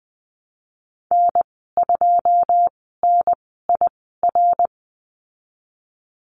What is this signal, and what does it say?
Morse code, heard as a keyed tone, about 20 wpm, spaced as for 15 wpm, 700 Hz. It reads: N2NIR
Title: 167 Lesson 27g - Number 7 - Call Signs 20wpm